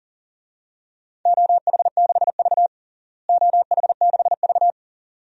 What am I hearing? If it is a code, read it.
Morse code, heard as a keyed tone, about 40 wpm, 700 Hz. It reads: OH6V OH6V
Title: DXCC-100 Call Signs - 4 Characters - Repeated 2x - 40wpm